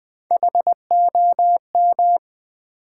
Morse code, heard as a keyed tone, about 20 wpm, 700 Hz. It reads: HOM
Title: Build A Word - 3 Letters At A Time - 8-Letter Words - 20wpm